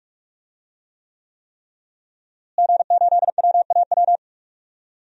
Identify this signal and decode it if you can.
Morse code, heard as a keyed tone, about 41 wpm, 700 Hz. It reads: G8WAW
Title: DXCC-100 Call Signs - 5 Characters - Repeated 3x - 45wpm